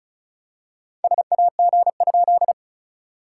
Morse code, heard as a keyed tone, about 36 wpm, 690 Hz. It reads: SAG?
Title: SOTA QSO Practice 35wpm